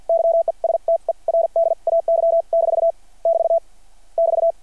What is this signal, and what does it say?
Morse code, heard as a keyed tone, about 53 wpm, spaced as for 41 wpm, 670 Hz. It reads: YESTERDAY- <BT> <BT>
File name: archived / CRSnet